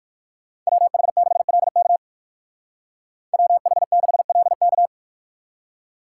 Morse code, heard as a keyed tone, about 55 wpm, 690 Hz. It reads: WH6LK WH6LK
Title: Speed Racing - USA 2x2 Call Signs 45wpm